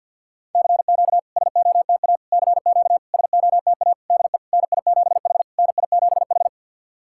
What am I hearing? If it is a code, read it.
Morse code, heard as a keyed tone, about 50 wpm, 700 Hz. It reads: CQ SOTA CQ SOTA DE NI7H NI7H
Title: SOTA QSO Practice 50wpm